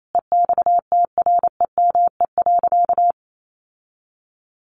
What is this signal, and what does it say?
Morse code, heard as a keyed tone, about 28 wpm, 700 Hz. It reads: EXTREME.
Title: Sentences from Top 1500 Words 28wpm